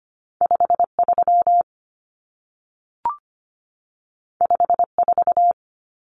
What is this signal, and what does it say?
Morse code, heard as a keyed tone, about 25 wpm, 700 Hz. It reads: 5 3 5 4
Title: Numbers - 2-Digit 25wpm